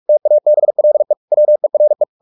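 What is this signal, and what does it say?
Morse code, heard as a keyed tone, about 44 wpm, 600 Hz. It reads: TABLE WERE
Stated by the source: Morse practice at 45 wpm, the novel Never by Ken Follett in English, from CW med Gnister.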